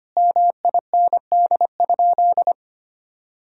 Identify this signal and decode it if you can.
Morse code, heard as a keyed tone, about 25 wpm, 700 Hz. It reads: MIND?
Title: Sentences from Top 1000 Words - Review Part 5 - 25wpm